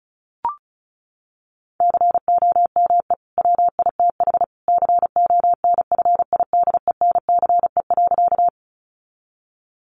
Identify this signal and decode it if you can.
Morse code, heard as a keyed tone, about 34 wpm, 700 Hz. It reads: COME WITH CONFIDENCE.
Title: Sentences from Top 1500 Words 35wpm